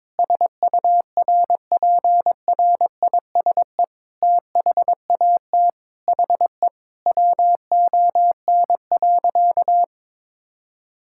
Morse code, heard as a keyed tone, about 21 wpm, 700 Hz. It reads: SURPRISE THAT HE WON.
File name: Sentences with 7-Word Phrases - 22wpm